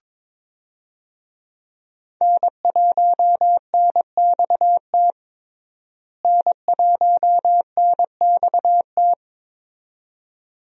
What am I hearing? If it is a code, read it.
Morse code, heard as a keyed tone, about 22 wpm, 700 Hz. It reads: N1NXT N1NXT
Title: USA Call Signs - 5 Characters - Repeated 2x - 22wpm